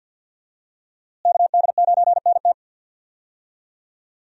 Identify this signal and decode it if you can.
Morse code, heard as a keyed tone, about 51 wpm, 700 Hz. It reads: KD9NT
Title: USA Call Signs - 5 Characters - Repeated 2x - 50wpm